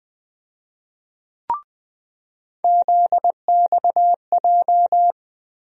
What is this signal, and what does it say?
Morse code, heard as a keyed tone, about 20 wpm, 700 Hz. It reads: ZXJ
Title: USA Call Sign Postfixes - 3 Letters 20wpm